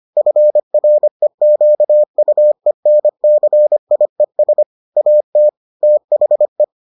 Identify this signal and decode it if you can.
Morse code, heard as a keyed tone, about 25 wpm, 600 Hz. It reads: FREQUENCIES AT THE